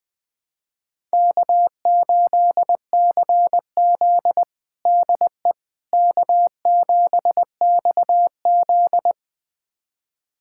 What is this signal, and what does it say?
Morse code, heard as a keyed tone, about 21 wpm, 700 Hz. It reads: K8CZ DE K7XZ
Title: USA Callsign DE Callsign - 4 Characters - 20wpm